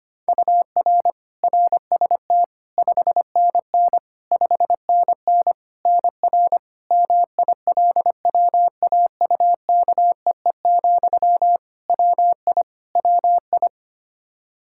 Morse code, heard as a keyed tone, about 25 wpm, 700 Hz. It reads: UR RST 5NN 5NN NR MILWAUKEE, WI WI
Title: Practice QSO - Exchange 1 - Beginning - 25wpm